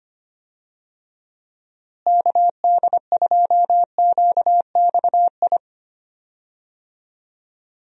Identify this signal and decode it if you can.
Morse code, heard as a keyed tone, about 25 wpm, 700 Hz. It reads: KD2QXI
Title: USA Call Signs - 6 Characters - Repeated 3x - 25wpm